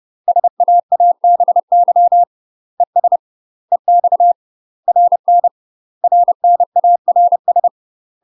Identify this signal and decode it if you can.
Morse code, heard as a keyed tone, about 30 wpm, 700 Hz. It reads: SAABY ES EX RN RNARS
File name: UK MASTER QSOs TRAINING / 30WPM / UK MASTER QSOs 0000